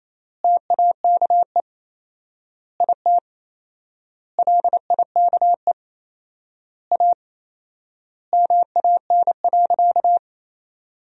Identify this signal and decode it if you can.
Morse code, heard as a keyed tone, about 28 wpm, 700 Hz. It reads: TAKE IT LIKE A MAN.
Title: 4x Word-Spacing - Sentences from Top 300 Words - 28wpm